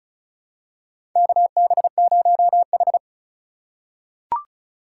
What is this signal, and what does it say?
Morse code, heard as a keyed tone, about 36 wpm, 700 Hz. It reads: KB0H
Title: USA Call Signs - 4 Characters - 3x Character Spacing - 35wpm